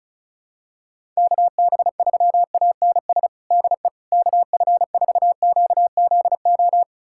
Morse code, heard as a keyed tone, about 36 wpm, 700 Hz. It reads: KB3ANS DE KF4QZO
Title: USA Callsign DE Callsign - 6 Characters - 35wpm